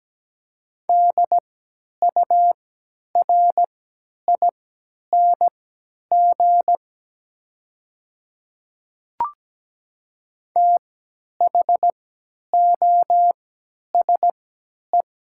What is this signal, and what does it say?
Morse code, heard as a keyed tone, about 17 wpm, spaced as for 10 wpm, 700 Hz. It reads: DURING THOSE
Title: 3x Character Spacing - Top 500 Words - 17wpm